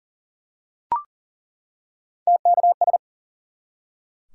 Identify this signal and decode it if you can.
Morse code, heard as a keyed tone, about 39 wpm, 700 Hz. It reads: TKS
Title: Top 180 QSO Elements 40wpm